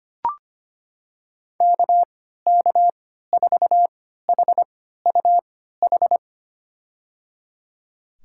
Morse code, heard as a keyed tone, about 25 wpm, spaced as for 14 wpm, 700 Hz. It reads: KK4HUH
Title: USA Call Signs - 6 Characters - 3x Character Spacing - 25wpm